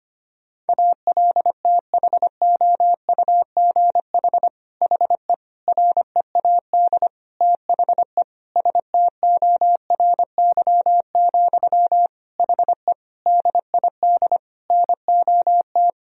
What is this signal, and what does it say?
Morse code, heard as a keyed tone, about 25 wpm, 700 Hz. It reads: ALTHOUGH HE READ THE STORY, HE DID NOT
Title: Sentences from Top 1000 Words - Review Part 3 - 25wpm